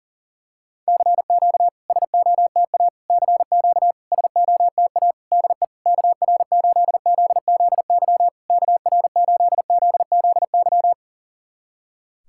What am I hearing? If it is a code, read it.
Morse code, heard as a keyed tone, about 40 wpm, 700 Hz. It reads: CQ SOTA CQ SOTA DE KR8ZZY KR8ZZY
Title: SOTA QSO Practice 40wpm